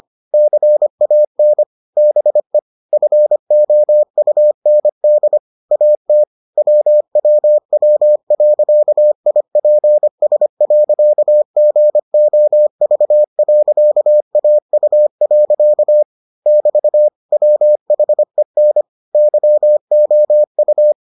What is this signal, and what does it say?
Morse code, heard as a keyed tone, about 25 wpm, 600 Hz. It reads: CAN BE FOUND AT WWW.IPS.GOV.AU. <BT> WHEN YOU